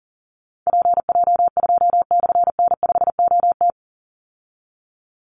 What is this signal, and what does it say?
Morse code, heard as a keyed tone, about 37 wpm, 700 Hz. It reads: PJ2/N5OT
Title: Call Signs - 40wpm